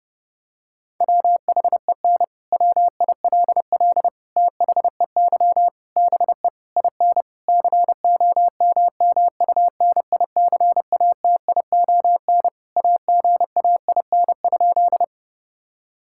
Morse code, heard as a keyed tone, about 32 wpm, 700 Hz. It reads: WHEN WILL THEY BE IN COMMUNICATION AGAIN?
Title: Sentences from Top 1000 Words - Review Part 6 - 30wpm